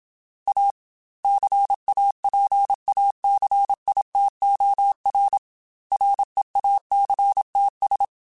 Morse code, heard as a keyed tone, about 27 wpm, 800 Hz. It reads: A CAPACITOR REACTS
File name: Technical Subjects / Technical Subjects-mp3s / XC @25